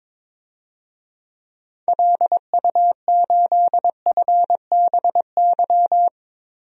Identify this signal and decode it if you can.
Morse code, heard as a keyed tone, about 22 wpm, 700 Hz. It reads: LU8FBY
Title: DXCC-100 Call Signs - 6 Characters - 22wpm